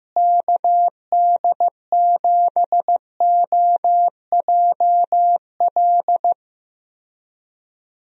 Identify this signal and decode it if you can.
Morse code, heard as a keyed tone, about 15 wpm, 700 Hz. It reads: KD7OJL
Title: USA Call Signs - 6 Characters - Repeated 2x - 15wpm